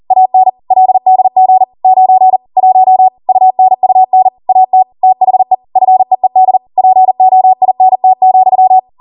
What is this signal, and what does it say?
Morse code, heard as a keyed tone, about 40 wpm, 750 Hz. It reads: AN LDG 9 1 UNUN AT THE FEED POINT,